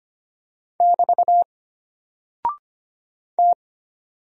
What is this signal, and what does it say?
Morse code, heard as a keyed tone, about 25 wpm, 700 Hz. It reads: <BT> T
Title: Single Character (Letter:Number:Punctuation:Prosign) Alt 25wpm